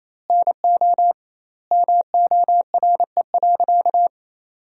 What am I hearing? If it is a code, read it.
Morse code, heard as a keyed tone, about 28 wpm, 700 Hz. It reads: NO MORE.
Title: 2x Word-Spacing - Sentences from Top 300 Words - 28wpm